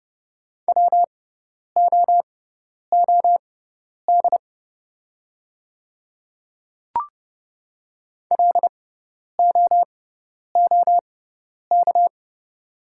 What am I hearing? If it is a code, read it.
Morse code, heard as a keyed tone, about 30 wpm, spaced as for 10 wpm, 690 Hz. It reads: WOOD LOOK
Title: 6x Character Spacing - Top 300 Words - 30wpm